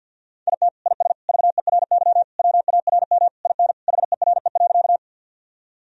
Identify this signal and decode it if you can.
Morse code, heard as a keyed tone, about 48 wpm, 700 Hz. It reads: IT IS VERY WARM IN HERE.
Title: Sentences from Top 1000 Words - Review Part 2 - 50wpm